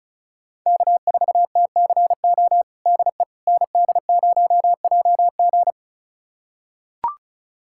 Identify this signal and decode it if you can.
Morse code, heard as a keyed tone, about 34 wpm, 690 Hz. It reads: K4TCO DE ND0JG
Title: USA Callsign DE Callsign - 5 Characters - 35wpm